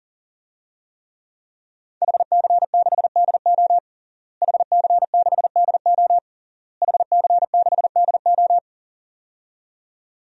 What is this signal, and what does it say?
Morse code, heard as a keyed tone, about 40 wpm, 700 Hz. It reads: HC6DO HC6DO HC6DO
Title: DXCC-100 Call Signs - 5 Characters - Repeated 3x - 40wpm